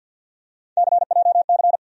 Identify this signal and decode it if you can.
Morse code, heard as a keyed tone, about 47 wpm, 700 Hz. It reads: CJC
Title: USA Call Sign Postfixes - 3 Letters 50wpm